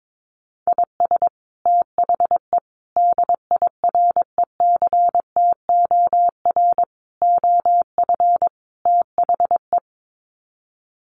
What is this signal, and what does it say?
Morse code, heard as a keyed tone, about 22 wpm, 700 Hz. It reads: IS THE DIRECTOR OF THE
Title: Sets of 5 Words Encore2 22wpm